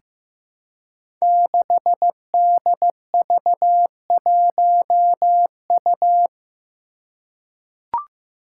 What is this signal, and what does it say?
Morse code, heard as a keyed tone, about 15 wpm, 700 Hz. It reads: 6DV1U
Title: Five Challenging Letter-Number Combinations 15wpm